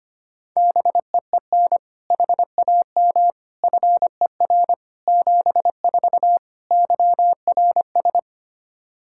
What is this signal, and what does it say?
Morse code, heard as a keyed tone, about 25 wpm, 700 Hz. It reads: BEEN HAM FER 74 YRS
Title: Practice QSO - Exchange 3 - More Info - 25wpm